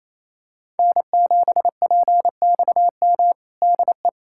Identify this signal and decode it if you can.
Morse code, heard as a keyed tone, about 28 wpm, 700 Hz. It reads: N7PXM DE
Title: USA Callsign DE Callsign - 5 Characters - 28wpm